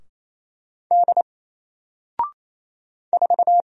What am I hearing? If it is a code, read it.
Morse code, heard as a keyed tone, about 27 wpm, 700 Hz. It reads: D 4
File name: Single-Letter-Number-28wpm